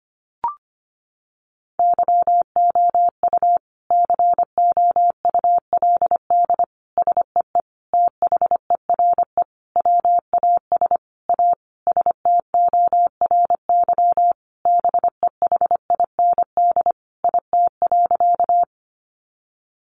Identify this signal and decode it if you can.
Morse code, heard as a keyed tone, about 25 wpm, 700 Hz. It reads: YOU COULD SEE THERE WAS A STORY BEHIND IT.
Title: Sentences from Top 500 Words 25wpm